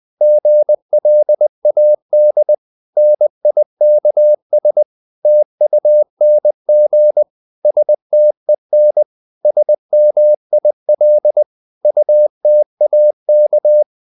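Morse code, heard as a keyed tone, about 20 wpm, 600 Hz. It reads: GLAD NIKS TUNG STEN SMIL UTAK
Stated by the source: Morse code practice of short Danish words at 20 wpm, from CW med Gnister.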